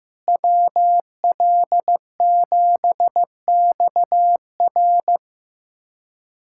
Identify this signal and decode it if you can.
Morse code, heard as a keyed tone, about 15 wpm, 700 Hz. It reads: WL7XR
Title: USA Callsign DE Callsign - 5 Characters - 15wpm